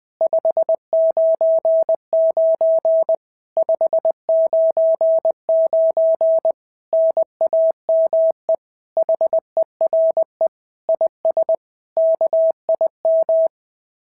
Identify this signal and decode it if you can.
Morse code, heard as a keyed tone, about 20 wpm, 650 Hz. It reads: 599 599 NAME HERE IS KIM
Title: SOTA QSO Practice 20wpm